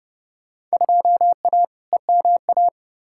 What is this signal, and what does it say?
Morse code, heard as a keyed tone, about 31 wpm, 700 Hz. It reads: 2A EMA
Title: Field Day Exchanges Encore - 30wpm